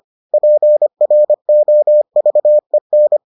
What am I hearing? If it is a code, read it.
Morse code, heard as a keyed tone, about 25 wpm, 600 Hz. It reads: PROVEN